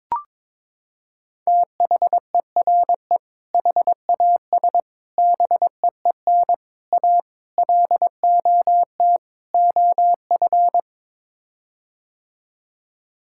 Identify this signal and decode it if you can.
Morse code, heard as a keyed tone, about 22 wpm, 700 Hz. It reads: THERE HAS BEEN A LOT OF
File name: practice / 22wpm / Sets of 6 Words - 22wpm